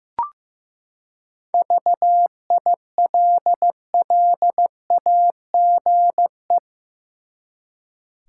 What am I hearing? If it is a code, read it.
Morse code, heard as a keyed tone, about 15 wpm, 700 Hz. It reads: VILLAGE
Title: Seven Letter Words - 15wpm